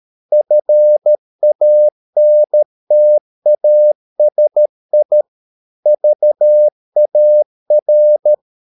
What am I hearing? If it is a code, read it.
Morse code, heard as a keyed tone, about 13 wpm, 600 Hz. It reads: FANTASI VAR
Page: Ferie 13 wpm